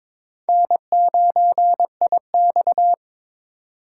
Morse code, heard as a keyed tone, about 22 wpm, 700 Hz. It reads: N9IX
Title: USA Call Signs - 4 Characters - 4x Character Spacing - 22wpm